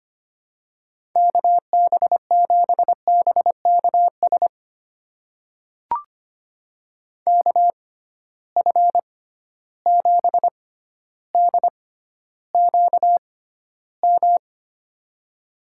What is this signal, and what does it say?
Morse code, heard as a keyed tone, about 25 wpm, spaced as for 13 wpm, 700 Hz. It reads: KB7BKS KF7DQM
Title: USA Call Signs - 6 Characters - 6x Character Spacing - 25wpm